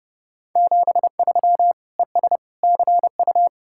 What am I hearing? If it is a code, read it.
Morse code, heard as a keyed tone, about 30 wpm, 700 Hz. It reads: 73 ES CU